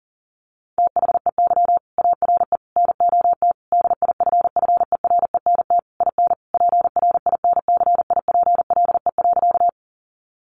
Morse code, heard as a keyed tone, about 40 wpm, 700 Hz. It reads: THEY ARE NOT DIFFERENT IN PRINCIPLE.
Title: Sentences from Top 1400 Words 40wpm